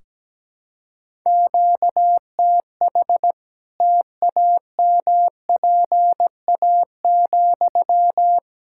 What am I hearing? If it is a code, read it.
Morse code, heard as a keyed tone, about 17 wpm, 700 Hz. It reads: QTH TAMPA,
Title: Practice QSO - Exchange 1 - Beginning - 17wpm